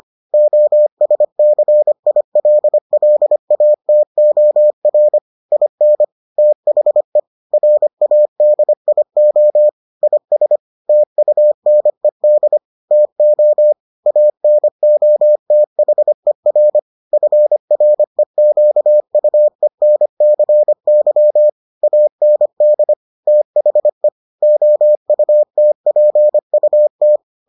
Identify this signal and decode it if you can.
Morse code, heard as a keyed tone, about 25 wpm, 600 Hz. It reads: OSCILLATOR IN THE RADIO IS TUNED TO ANOTHER FREQUENCY AND THE OUTPUT